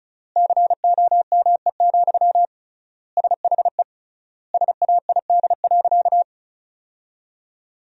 Morse code, heard as a keyed tone, about 36 wpm, 700 Hz. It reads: COME, SHE SAID.
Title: 3x Word-Spacing - Sentences from Top 100 Words - 35wpm